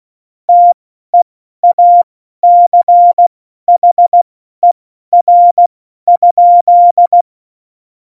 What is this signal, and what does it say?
Morse code, heard as a keyed tone, about 16 wpm, spaced as for 12 wpm, 700 Hz. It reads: TEACHER?
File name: Morsekurs / Witze